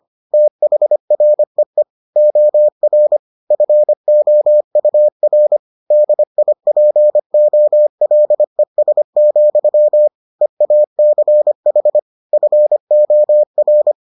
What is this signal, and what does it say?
Morse code, heard as a keyed tone, about 25 wpm, 600 Hz. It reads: THREE OR FOUR DIPOLES, EACH FOR